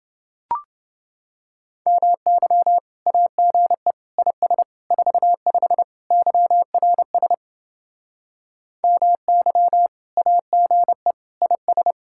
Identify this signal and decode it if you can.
Morse code, heard as a keyed tone, about 29 wpm, 700 Hz. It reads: MY AGE IS 45 YRS MY AGE IS